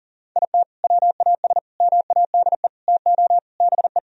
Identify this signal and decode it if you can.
Morse code, heard as a keyed tone, about 40 wpm, 700 Hz. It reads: IT WAS MADE TO BE
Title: Sentences from Top 1000 Words - Review Part 4 - 40wpm